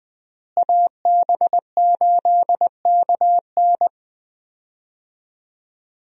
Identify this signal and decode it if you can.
Morse code, heard as a keyed tone, about 20 wpm, 700 Hz. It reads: AB8KN
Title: USA Call Signs - 5 Characters - Repeated 2x - 20wpm